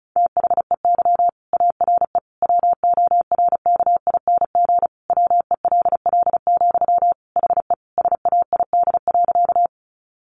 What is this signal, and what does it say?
Morse code, heard as a keyed tone, about 34 wpm, 700 Hz. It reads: THEY ARE WORKING WELL, HE SAID.
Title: Sentences from Top 700 Words 35wpm